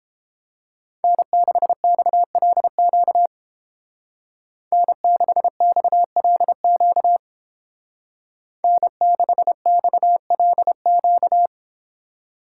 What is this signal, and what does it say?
Morse code, heard as a keyed tone, about 30 wpm, 700 Hz. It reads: N6XLQ N6XLQ N6XLQ